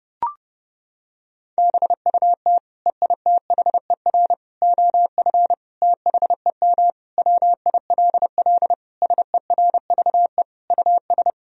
Morse code, heard as a keyed tone, about 30 wpm, 700 Hz. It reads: BUT EITHER OF THEM WILL SERVE US
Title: Sentences from Top 1000 Words - Review Part 4 - 30wpm